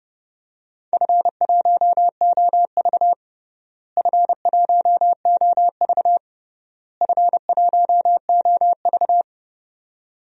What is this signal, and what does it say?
Morse code, heard as a keyed tone, about 30 wpm, 700 Hz. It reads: F1OV F1OV F1OV
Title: DXCC-100 Call Signs - 4 Characters - Repeated 3x - 30wpm